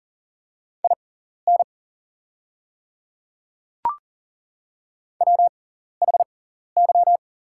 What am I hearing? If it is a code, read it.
Morse code, heard as a keyed tone, about 40 wpm, spaced as for 14 wpm, 690 Hz. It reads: IN WHY
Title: 6x Character Spacing - Top 200 Words - 40wpm